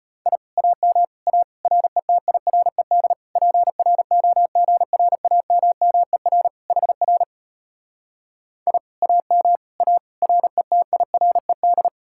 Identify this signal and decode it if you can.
Morse code, heard as a keyed tone, about 33 wpm, 700 Hz. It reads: I AM A RETIRED PROGRAMMER HR I AM A RETIRED